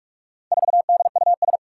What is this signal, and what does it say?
Morse code, heard as a keyed tone, about 44 wpm, 700 Hz. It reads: 4DUS
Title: Four Challenging Letter-Number Combinations 45wpm